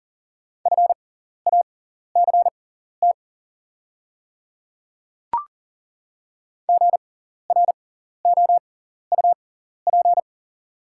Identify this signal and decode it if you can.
Morse code, heard as a keyed tone, about 40 wpm, spaced as for 14 wpm, 700 Hz. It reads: FACT GROUP
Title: 6x Character Spacing - Top 500 Words - 40wpm